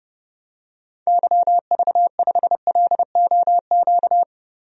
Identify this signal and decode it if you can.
Morse code, heard as a keyed tone, about 30 wpm, 700 Hz. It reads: YV5LOQ
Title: DXCC-100 Call Signs - 6 Characters - Repeated 2x - 30wpm